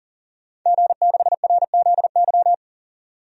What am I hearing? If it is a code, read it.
Morse code, heard as a keyed tone, about 40 wpm, 700 Hz. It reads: G6RZY
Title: DXCC-100 Call Signs - 5 Characters - 40wpm